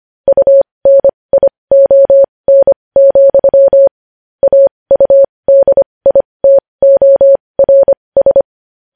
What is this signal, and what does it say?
Morse code, heard as a keyed tone, about 25 wpm, spaced as for 20 wpm, 550 Hz. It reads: UNION, AUDITORS
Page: News Headlines in Morse Code at 20 WPM